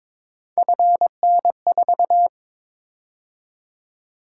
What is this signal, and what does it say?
Morse code, heard as a keyed tone, about 22 wpm, 700 Hz. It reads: FN4
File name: practice / 22wpm / Three Letter-Number Combinations - 22wpm